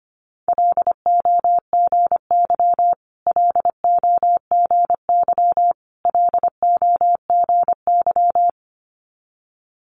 Morse code, heard as a keyed tone, about 25 wpm, 700 Hz. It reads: LOGY LOGY LOGY
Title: Multi-letter Phonemes Repeated 3x 25wpm